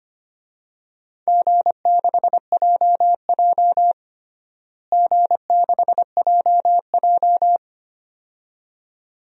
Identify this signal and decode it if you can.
Morse code, heard as a keyed tone, about 25 wpm, 700 Hz. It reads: G6JJ G6JJ
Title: DXCC-100 Call Signs - 4 Characters - Repeated 2x - 25wpm